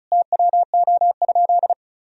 Morse code, heard as a keyed tone, about 34 wpm, 700 Hz. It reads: TWO?
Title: Sentences from Top 1000 Words - Review Part 4 - 35wpm